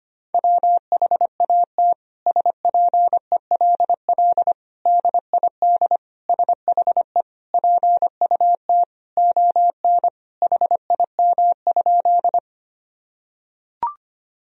Morse code, heard as a keyed tone, about 25 wpm, 700 Hz. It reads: WHAT SPELL DID SHE PUT ON HIM?
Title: Sentences from Top 1000 Words - Review Part 1 - 25wpm